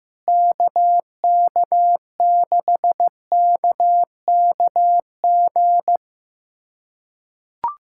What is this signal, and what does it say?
Morse code, heard as a keyed tone, about 15 wpm, 700 Hz. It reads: KK6KKG
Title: USA Callsign DE Callsign - 6 Characters - 15wpm